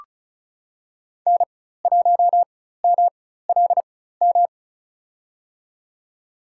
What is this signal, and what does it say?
Morse code, heard as a keyed tone, about 34 wpm, spaced as for 16 wpm, 700 Hz. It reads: N1MLM
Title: USA Call Signs - 4x Character Spacing - 35wpm